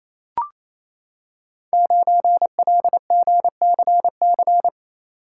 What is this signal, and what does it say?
Morse code, heard as a keyed tone, about 28 wpm, 700 Hz. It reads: 9LGCC
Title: Code Groups - 5 Letter:Numbers - 28wpm